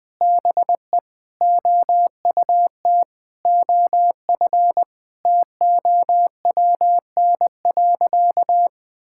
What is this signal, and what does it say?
Morse code, heard as a keyed tone, about 20 wpm, 700 Hz. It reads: BE OUT OF TOWN.
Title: Sentences with 3-Word Phrases - 20wpm